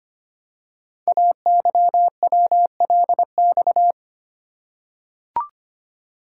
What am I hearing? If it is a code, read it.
Morse code, heard as a keyed tone, about 25 wpm, 700 Hz. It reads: AYWLX
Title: Code Groups - 5 Letters - 25wpm